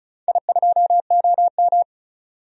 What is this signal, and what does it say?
Morse code, heard as a keyed tone, about 36 wpm, 700 Hz. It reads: I2OM
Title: DXCC-100 Call Signs - 4 Characters - 35wpm